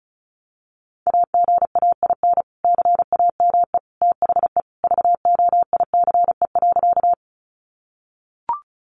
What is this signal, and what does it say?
Morse code, heard as a keyed tone, about 34 wpm, 700 Hz. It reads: AGAIN CAME THE VOICE.
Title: Sentences from Top 500 Words 35wpm